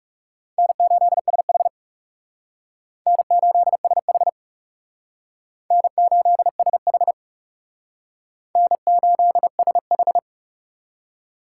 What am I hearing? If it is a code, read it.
Morse code, heard as a keyed tone, about 37 wpm, 700 Hz. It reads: N8SH N8SH N8SH N8SH
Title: Speed Racing - USA 1x2 Call Signs 30wpm